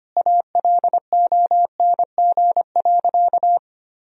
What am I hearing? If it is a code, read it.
Morse code, heard as a keyed tone, about 25 wpm, 700 Hz. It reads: ALONG.
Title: Sentences from Top 1000 Words - Review Part 1 - 25wpm